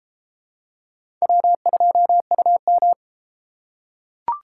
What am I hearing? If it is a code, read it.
Morse code, heard as a keyed tone, about 33 wpm, 700 Hz. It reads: W2UM